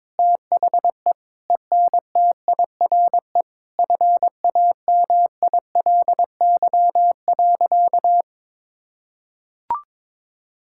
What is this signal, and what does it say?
Morse code, heard as a keyed tone, about 22 wpm, 700 Hz. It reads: THE ENTIRE FAMILY.
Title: Sentences from Top 1000 Words - Review Part 5 - 22wpm